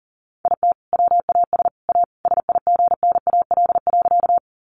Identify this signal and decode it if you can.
Morse code, heard as a keyed tone, about 40 wpm, 700 Hz. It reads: IT WAS A SIGNAL.
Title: Sentences from Top 1400 Words 40wpm